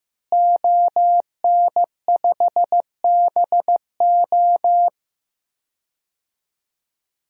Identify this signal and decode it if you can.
Morse code, heard as a keyed tone, about 15 wpm, 700 Hz. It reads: ON5BO
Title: DXCC-100 Call Signs - 5 Characters - Repeated 2x - 15wpm